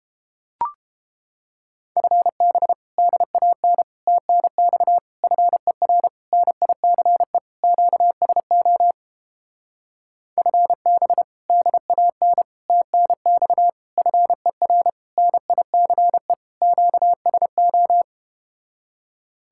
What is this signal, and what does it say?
Morse code, heard as a keyed tone, about 30 wpm, 700 Hz. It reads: FB DAN TNX FER NICE QSO FB DAN TNX FER NICE QSO